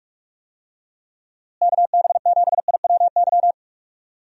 Morse code, heard as a keyed tone, about 46 wpm, 700 Hz. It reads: KB7IWY
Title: USA Call Signs - 6 Characters - Repeated 3x - 45wpm